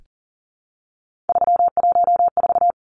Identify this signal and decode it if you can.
Morse code, heard as a keyed tone, about 37 wpm, 700 Hz. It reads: 314
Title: Numbers - 3-Digit 40wpm